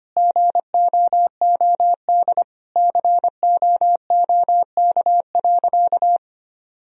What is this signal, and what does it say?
Morse code, heard as a keyed tone, about 25 wpm, 700 Hz. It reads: GOOD COOK.
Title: Sentences from Top 1000 Words - Review Part 4 - 25wpm